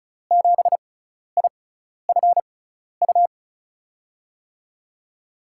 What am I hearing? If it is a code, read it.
Morse code, heard as a keyed tone, about 36 wpm, spaced as for 12 wpm, 700 Hz. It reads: 7IFU
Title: USA Call Signs - 6 Characters - 6x Character Spacing - 35wpm